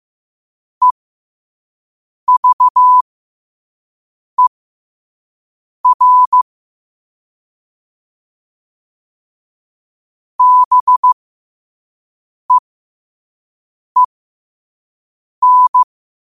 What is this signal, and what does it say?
Morse code, heard as a keyed tone, about 15 wpm, spaced as for 5 wpm, 1000 Hz. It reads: EVER BEEN
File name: qotdfiles / 5wpm